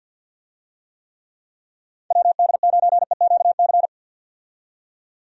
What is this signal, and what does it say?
Morse code, heard as a keyed tone, about 53 wpm, 700 Hz. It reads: WD9EQC
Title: USA Call Signs - 6 Characters - 50wpm